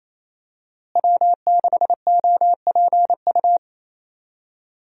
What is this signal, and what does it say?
Morse code, heard as a keyed tone, about 28 wpm, 700 Hz. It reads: W6OPU
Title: USA Call Signs - 5 Characters - Repeated 3x - 28wpm